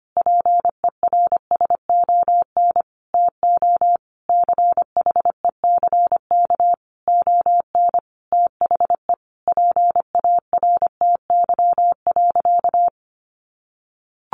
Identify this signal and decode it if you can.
Morse code, heard as a keyed tone, about 25 wpm, 700 Hz. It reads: PERSON TO CHECK ON THE PARTY.
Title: Sentences from Top 1400 Words 25wpm